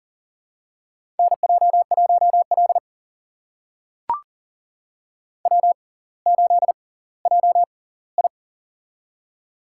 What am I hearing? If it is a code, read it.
Morse code, heard as a keyed tone, about 40 wpm, spaced as for 20 wpm, 700 Hz. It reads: NJ1L W8JI
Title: USA Call Signs - 4 Characters - 6x Character Spacing - 40wpm